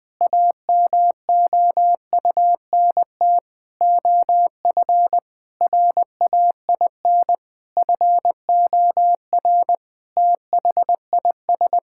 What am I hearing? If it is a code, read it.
Morse code, heard as a keyed tone, about 21 wpm, 700 Hz. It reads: AMOUNT OF RAIN FOR THIS
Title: Sentences from Top 1000 Words - Review Part 6 - 20wpm